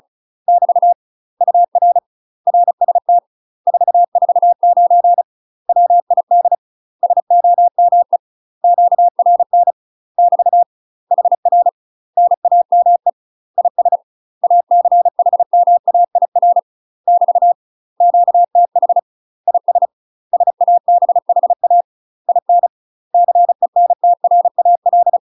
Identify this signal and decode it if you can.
Morse code, heard as a keyed tone, about 36 wpm, 700 Hz. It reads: <BT> UR RST 449 WID SOME QRN <BT> HR NAME IS ACHMAIR <BT> QTH IS SABHA IN CENTRAL
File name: UK MASTER QSOs TRAINING / 35WPM / UK MASTER QSOs 0028